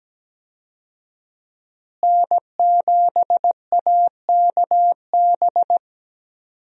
Morse code, heard as a keyed tone, about 17 wpm, 700 Hz. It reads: N7AKB
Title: USA Call Signs - 5 Characters - Repeated 2x - 17wpm